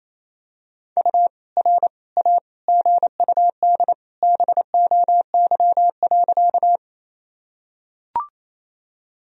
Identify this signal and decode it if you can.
Morse code, heard as a keyed tone, about 28 wpm, 700 Hz. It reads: U R A GUD BOY.
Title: Sentences from Top 200 Words and Abbreviations 28wpm